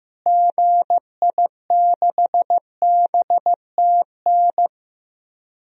Morse code, heard as a keyed tone, about 15 wpm, 700 Hz. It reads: GI6BTN
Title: DXCC-100 Call Signs - 6 Characters - Repeated 3x - 15wpm